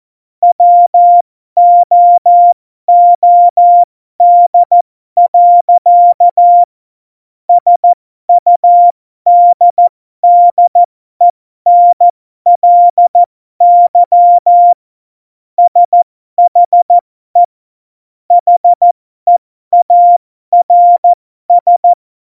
Morse code, heard as a keyed tone, about 14 wpm, spaced as for 12 wpm, 700 Hz. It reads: WOOD. SUDDENLY SHE HEARS
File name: Morsekurs / Witze